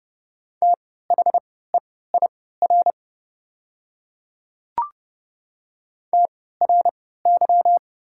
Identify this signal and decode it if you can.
Morse code, heard as a keyed tone, about 30 wpm, spaced as for 17 wpm, 700 Hz. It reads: THEIR TRY
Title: 3x Character Spacing - Top 200 Words - 30wpm